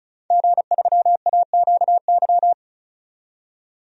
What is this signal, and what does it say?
Morse code, heard as a keyed tone, about 34 wpm, 700 Hz. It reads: G3AQY
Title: DXCC-100 Call Signs - 5 Characters - 35wpm